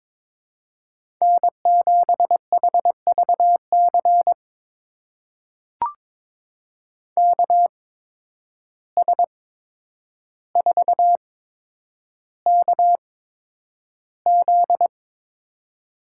Morse code, heard as a keyed tone, about 22 wpm, spaced as for 9 wpm, 700 Hz. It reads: N7HVC KS4KZ
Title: USA Call Signs - 5 Characters - 8x Character Spacing - 22wpm